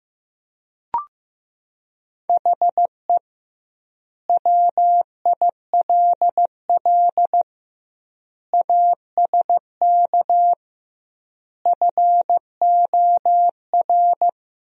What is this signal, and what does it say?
Morse code, heard as a keyed tone, about 15 wpm, 700 Hz. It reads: HE WILL ASK FOR
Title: 2x Word-Spacing - Sentences from Top 200 Words - 15wpm